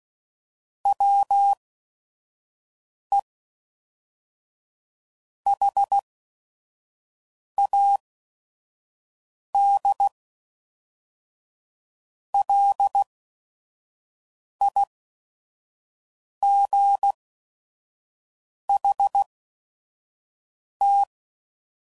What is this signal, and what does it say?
Morse code, heard as a keyed tone, about 16 wpm, spaced as for 5 wpm, 800 Hz. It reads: WE HAD LIGHT
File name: CW Code Groups @5WPM / Weather Sets / Weather 4 24